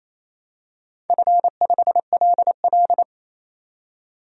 Morse code, heard as a keyed tone, about 29 wpm, 700 Hz. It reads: F5LL
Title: DXCC-100 Call Signs - 4 Characters - 28wpm